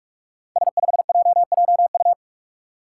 Morse code, heard as a keyed tone, about 44 wpm, 700 Hz. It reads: S5JJU
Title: Five Challenging Letter-Number Combinations 45wpm